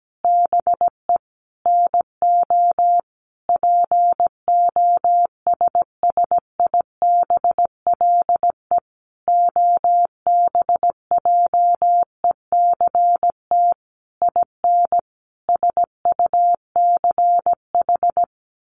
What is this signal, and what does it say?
Morse code, heard as a keyed tone, about 17 wpm, 700 Hz. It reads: BE NO POSSIBLE OBJECT IN SUCH
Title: Sentences from Top 1500 Words 17wpm